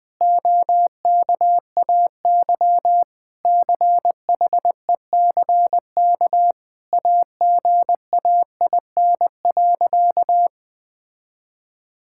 Morse code, heard as a keyed tone, about 21 wpm, 700 Hz. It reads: OKAY CHECK AGAIN.
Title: Sentences with 4-Word Phrases - Spoken Beginning - 20wpm